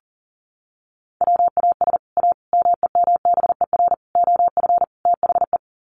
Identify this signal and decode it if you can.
Morse code, heard as a keyed tone, about 40 wpm, 700 Hz. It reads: WAS A MEMBER OF THE
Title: Sets of 5 Words 40wpm